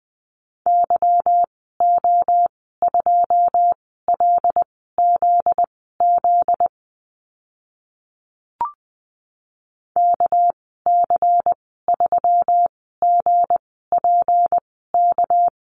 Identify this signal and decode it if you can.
Morse code, heard as a keyed tone, about 20 wpm, spaced as for 14 wpm, 700 Hz. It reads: YO2LZZ KC3GPK
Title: 245 Lesson 39g - Letter z - Call Signs 20wpm